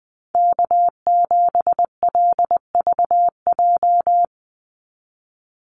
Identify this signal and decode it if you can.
Morse code, heard as a keyed tone, about 19 wpm, 700 Hz. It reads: K7LVJ
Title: Call Signs Repeated 2x - 20wpm